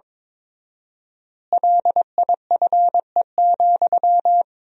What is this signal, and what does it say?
Morse code, heard as a keyed tone, about 22 wpm, 700 Hz. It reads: LIFE,
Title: 4x Word-Spacing - Sentences from Top 300 Words - 22wpm